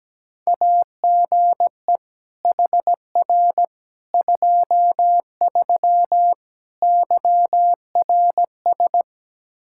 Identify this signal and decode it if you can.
Morse code, heard as a keyed tone, about 17 wpm, 700 Hz. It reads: AGE HR 23 YRS
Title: K1USN SST QSO Practice 17wpm